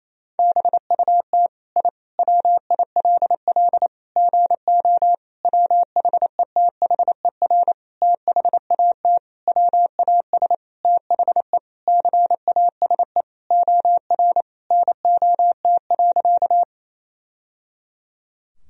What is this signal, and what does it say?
Morse code, heard as a keyed tone, about 28 wpm, 690 Hz. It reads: BUT I WILL GO WHETHER THAT WAS THE CASE OR NOT.
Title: Sentences from Top 1000 Words - Review Part 3 - 28wpm